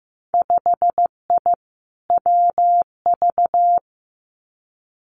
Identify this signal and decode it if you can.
Morse code, heard as a keyed tone, about 15 wpm, 700 Hz. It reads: 5I WV
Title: Winter Field Day Exchanges Encore 15wpm